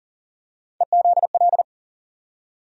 Morse code, heard as a keyed tone, about 40 wpm, 700 Hz. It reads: EZL
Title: USA Call Sign Postfixes - 3 Letters 40wpm